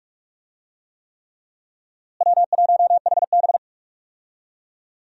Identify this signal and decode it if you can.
Morse code, heard as a keyed tone, about 42 wpm, 700 Hz. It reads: W1HB
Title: USA Call Signs - 4 Characters - Repeated 3x - 45wpm